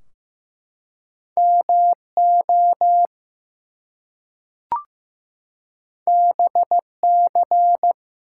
Morse code, heard as a keyed tone, about 15 wpm, 700 Hz. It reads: MO BC
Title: US States and Canadian Provinces 15wpm